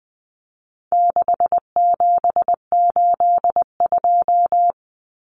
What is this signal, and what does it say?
Morse code, heard as a keyed tone, about 20 wpm, 700 Hz. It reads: 6782
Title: Common Cut Numbers - 4-Digit 20wpm